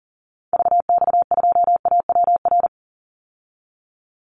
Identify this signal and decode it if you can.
Morse code, heard as a keyed tone, about 40 wpm, 700 Hz. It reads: VX2AWR
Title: Call Signs Repeated 2x - 40wpm